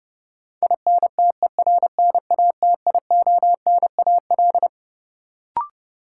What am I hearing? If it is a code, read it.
Morse code, heard as a keyed tone, about 31 wpm, 700 Hz. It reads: INTERNATIONAL
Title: Sentences from Top 1000 Words - Review Part 4 - New Word Drill - 30wpm